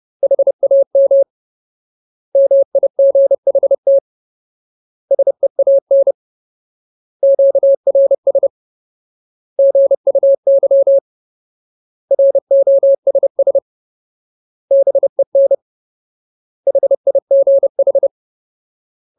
Morse code, heard as a keyed tone, about 30 wpm, 550 Hz. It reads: HAM MIGHT SEAN QRS GUY ROSS BEN HIGH
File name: Ultimate Word List / 30WPM Ultimate